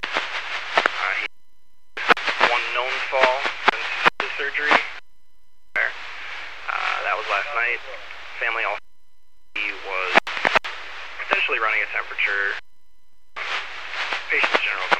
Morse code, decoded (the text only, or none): none